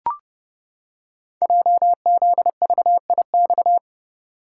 Morse code, heard as a keyed tone, about 29 wpm, 700 Hz. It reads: JZVIX
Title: Code Groups - 5 Letters - 30wpm